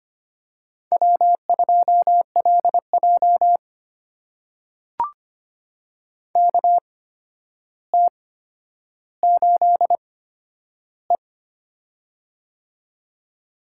Morse code, heard as a keyed tone, about 25 wpm, spaced as for 11 wpm, 700 Hz. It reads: W2LJ KT8E
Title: USA Call Signs - 4 Characters - 8x Character Spacing - 25wpm